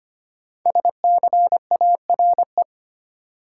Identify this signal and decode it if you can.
Morse code, heard as a keyed tone, about 25 wpm, 700 Hz. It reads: SCARE
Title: Sentences from Top 1000 Words - Review Part 5 - New Word Drill - 25wpm